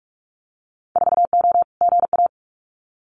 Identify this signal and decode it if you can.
Morse code, heard as a keyed tone, about 46 wpm, 700 Hz. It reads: 4O GA
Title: Winter Field Day Exchanges 45wpm